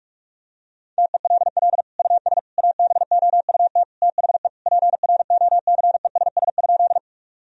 Morse code, heard as a keyed tone, about 46 wpm, 700 Hz. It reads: TELL US ABOUT THE PROCESS?
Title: Sentences from Top 1000 Words - Review Part 3 - 45wpm